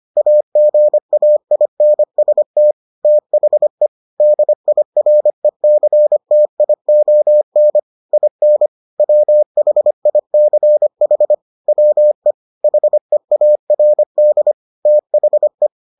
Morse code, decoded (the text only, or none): AGAINST THE DIRECTION IN WHICH WE HEARD THE